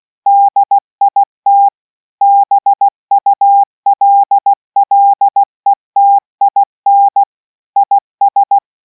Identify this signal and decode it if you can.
Morse code, heard as a keyed tone, about 16 wpm, 800 Hz. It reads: DIT BULLETIN IS